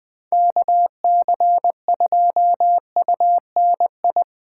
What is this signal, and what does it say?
Morse code, heard as a keyed tone, about 20 wpm, 700 Hz. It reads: KC2UNI
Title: Speed Racing - USA 2x3 Call Signs 17wpm